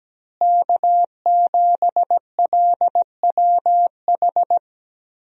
Morse code, heard as a keyed tone, about 17 wpm, 700 Hz. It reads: K7LWH
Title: USA Call Signs - 5 Characters - Repeated 3x - 17wpm